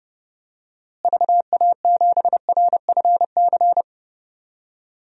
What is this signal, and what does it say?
Morse code, heard as a keyed tone, about 32 wpm, 700 Hz. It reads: VA7RFC
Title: DXCC-100 Call Signs - 6 Characters - Repeated 3x - 30wpm